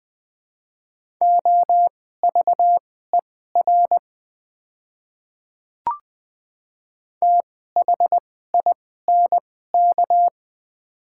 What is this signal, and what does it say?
Morse code, heard as a keyed tone, about 20 wpm, spaced as for 14 wpm, 700 Hz. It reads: OVER THINK
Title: 2x Character Spacing - Top 300 Words - 20wpm